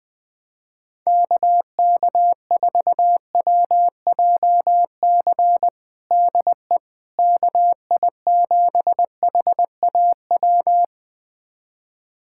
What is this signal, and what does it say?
Morse code, heard as a keyed tone, about 20 wpm, 700 Hz. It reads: KK4WJC DE KI7HAW
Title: USA Callsign DE Callsign - 6 Characters - 20wpm